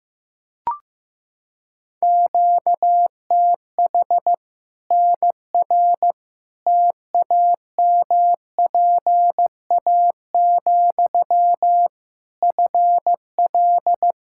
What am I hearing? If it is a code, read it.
Morse code, heard as a keyed tone, about 16 wpm, 700 Hz. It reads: QTH NR TAMPA, FL
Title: K1USN SST QSO Practice 15wpm